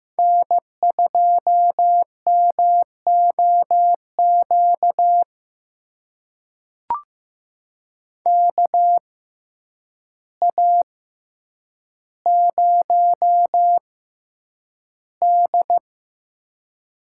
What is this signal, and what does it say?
Morse code, heard as a keyed tone, about 15 wpm, spaced as for 8 wpm, 700 Hz. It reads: N2MOQ KA0D
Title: USA Call Signs - 5 Characters - 6x Character Spacing - 15wpm